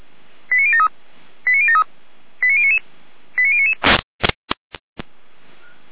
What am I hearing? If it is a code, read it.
Five-tone selective call (ZVEI), heard as quick runs of five tones, tones of about 70 ms. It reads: 89073 89073 89099 89099